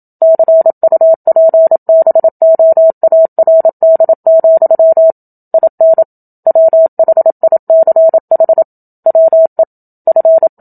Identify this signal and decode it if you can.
Morse code, heard as a keyed tone, about 27 wpm, 650 Hz. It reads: CUPBOARD, IN WHICH WE F
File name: Book Two / 01-Chapter 1 - 27 WPM